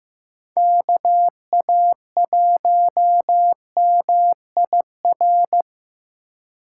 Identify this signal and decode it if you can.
Morse code, heard as a keyed tone, about 18 wpm, 700 Hz. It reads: KA1MIR
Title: Speed Racing - USA 2x3 Call Signs 15wpm